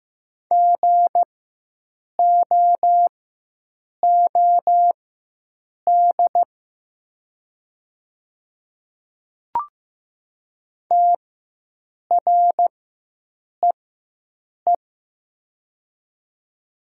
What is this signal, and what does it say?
Morse code, heard as a keyed tone, about 15 wpm, spaced as for 7 wpm, 700 Hz. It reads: GOOD TREE
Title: 4x Character Spacing - Top 300 Words - 15wpm